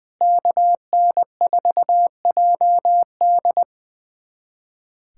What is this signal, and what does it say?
Morse code, heard as a keyed tone, about 20 wpm, 700 Hz. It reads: KN4JD
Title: USA Callsign DE Callsign - 5 Characters - 20wpm